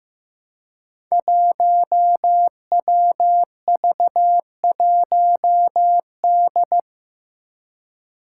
Morse code, heard as a keyed tone, about 15 wpm, 700 Hz. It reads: 1WV1D
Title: Five Challenging Letter-Number Combinations 15wpm